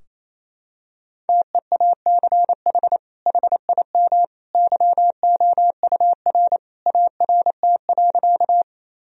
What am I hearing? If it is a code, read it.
Morse code, heard as a keyed tone, about 28 wpm, 700 Hz. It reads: TEACH HIM YOUR ART.
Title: Sentences from Top 1000 Words - Review Part 3 - 28wpm